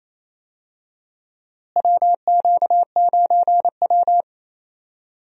Morse code, heard as a keyed tone, about 28 wpm, 700 Hz. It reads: WQ9W
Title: USA Call Signs - 4 Characters - Repeated 2x - 28wpm